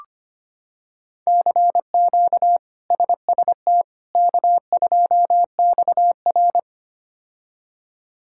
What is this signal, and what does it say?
Morse code, heard as a keyed tone, about 25 wpm, 700 Hz. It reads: CQ SST K2XR
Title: K1USN SST QSO Practice - 25wpm